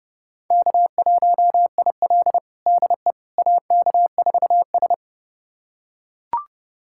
Morse code, heard as a keyed tone, about 32 wpm, 700 Hz. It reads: K1IL DE AK4S
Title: USA Callsign DE Callsign - 4 Characters - 30wpm